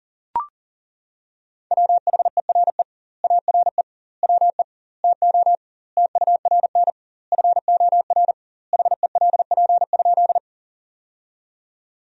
Morse code, heard as a keyed tone, about 37 wpm, 700 Hz. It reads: WHERE ARE WE TO TURN FOR HELP?
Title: 2x Word-Spacing - Sentences from Top 200 Words - 40wpm